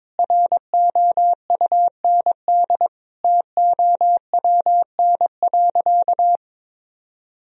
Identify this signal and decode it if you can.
Morse code, heard as a keyed tone, about 22 wpm, 700 Hz. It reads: ROUND TOWN.
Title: Sentences from Top 1000 Words - Review Part 1 - 22wpm